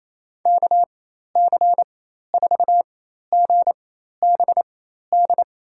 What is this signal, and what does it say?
Morse code, heard as a keyed tone, about 28 wpm, spaced as for 13 wpm, 700 Hz. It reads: KC4GBD
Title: USA Call Signs - 6 Characters - 4x Character Spacing - 28wpm